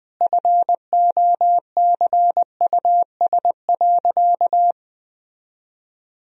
Morse code, heard as a keyed tone, about 21 wpm, 700 Hz. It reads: FOCUS.
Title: Sentences from Top 1000 Words - Review Part 4 - 20wpm